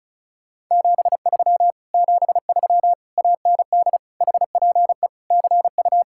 Morse code, heard as a keyed tone, about 40 wpm, 700 Hz. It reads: 73 73 AND HPE CU